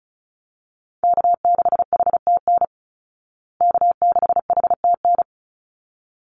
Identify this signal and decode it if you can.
Morse code, heard as a keyed tone, about 34 wpm, 700 Hz. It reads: K6HTN K6HTN
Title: Call Signs Repeated 3x - 35wpm